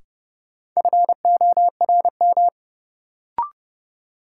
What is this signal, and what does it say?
Morse code, heard as a keyed tone, about 31 wpm, 700 Hz. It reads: FORM
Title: Sentences from Top 1000 Words - Review Part 1 - New Word Drill - 30wpm